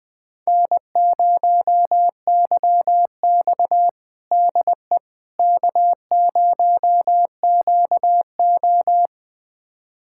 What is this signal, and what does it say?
Morse code, heard as a keyed tone, about 21 wpm, 700 Hz. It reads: N0YX DE K0QO
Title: USA Callsign DE Callsign - 4 Characters - 20wpm